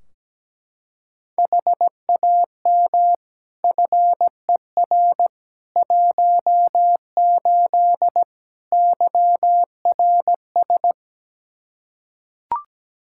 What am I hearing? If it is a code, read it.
Morse code, heard as a keyed tone, about 17 wpm, 700 Hz. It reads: HAM FER 18 YRS
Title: Practice QSO - Exchange 3 - More Info - 17wpm